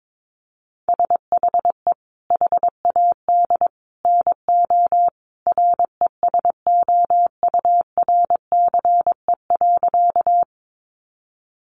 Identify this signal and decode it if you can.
Morse code, heard as a keyed tone, about 22 wpm, 700 Hz. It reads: SHE HAD NO RESOURCE.
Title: Sentences from Top 1300 Words 22wpm